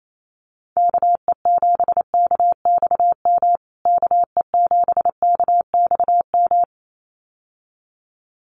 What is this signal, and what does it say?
Morse code, heard as a keyed tone, about 28 wpm, 700 Hz. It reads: KE7KXM KE7KXM
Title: USA Call Signs Repeated 2x 28wpm